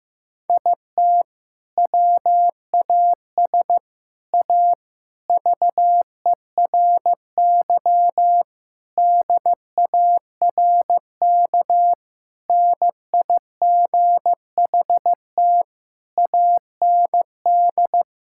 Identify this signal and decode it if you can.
Morse code, heard as a keyed tone, about 16 wpm, 700 Hz. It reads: IT WAS A VERY DARK NIGHT AND
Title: Sentences from Top 1600 Words 15wpm